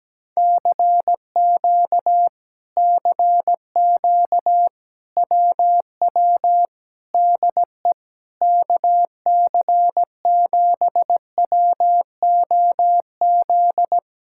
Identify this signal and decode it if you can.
Morse code, heard as a keyed tone, about 17 wpm, 700 Hz. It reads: CQ CQ WW DE KC7WOZ
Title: Speed-Racing Common QSO Phrases Encore - 17wpm